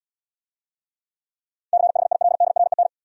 Morse code, heard as a keyed tone, about 70 wpm, 700 Hz. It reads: CHILDREN